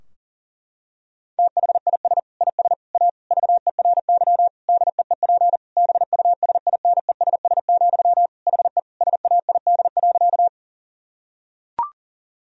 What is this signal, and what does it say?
Morse code, heard as a keyed tone, about 40 wpm, 700 Hz. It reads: THIS IS A VERY DEEP BUSINESS, HE SAID.
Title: Sentences from Top 1000 Words - Review Part 3 - 40wpm